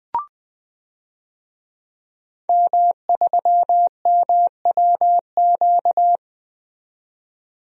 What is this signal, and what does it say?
Morse code, heard as a keyed tone, about 20 wpm, 700 Hz. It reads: M3MWQ
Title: DXCC-100 Call Signs - 5 Characters - 20wpm